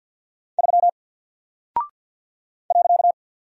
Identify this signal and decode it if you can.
Morse code, heard as a keyed tone, about 49 wpm, 700 Hz. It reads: 3 .
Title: Single Character (Letter:Number:Punctuation:Prosign) 50wpm